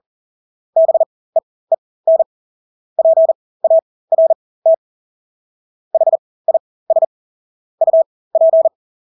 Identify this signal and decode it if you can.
Morse code, heard as a keyed tone, about 40 wpm, spaced as for 20 wpm, 650 Hz. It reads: BEEN PART HIS UP
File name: Engelske ord 40-20 wpm